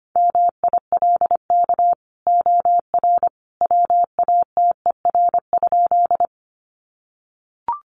Morse code, heard as a keyed tone, about 25 wpm, 700 Hz. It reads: MILK OR WATER?
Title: Sentences from Top 1500 Words 25wpm